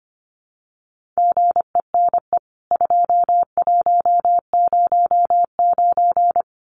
Morse code, heard as a keyed tone, about 25 wpm, 700 Hz. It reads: GENE 2109
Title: CWT Exchange 25wpm